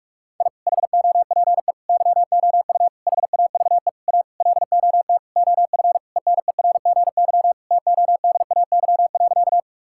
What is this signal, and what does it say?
Morse code, heard as a keyed tone, about 46 wpm, 700 Hz. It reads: I HOPE YOU HAVE A LOT OF ENERGY TODAY.
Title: Sentences from Top 1000 Words - Review Part 6 - 45wpm